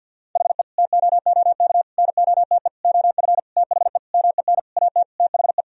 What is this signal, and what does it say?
Morse code, heard as a keyed tone, about 52 wpm, 700 Hz. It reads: HE TOOK NOTE OF THE MEN AT THE
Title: Sentences from Top 1000 Words - Review Part 2 - 50wpm